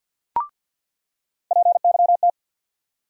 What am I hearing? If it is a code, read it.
Morse code, heard as a keyed tone, about 46 wpm, 700 Hz. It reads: PYT
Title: USA Call Sign Postfixes - 3 Letters 50wpm